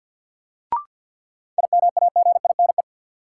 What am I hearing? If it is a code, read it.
Morse code, heard as a keyed tone, about 53 wpm, 700 Hz. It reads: IMAGINE
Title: Sentences from Top 1400 Words - New Word Drill - 50wpm